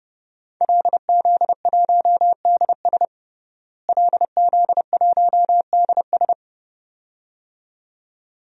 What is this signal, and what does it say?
Morse code, heard as a keyed tone, about 32 wpm, 700 Hz. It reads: LZ1DS LZ1DS
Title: DXCC-100 Call Signs - 5 Characters - Repeated 2x - 30wpm